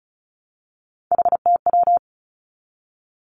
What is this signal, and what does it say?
Morse code, heard as a keyed tone, about 36 wpm, 700 Hz. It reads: HTW
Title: Three Challenging Letter-Number Combinations 35wpm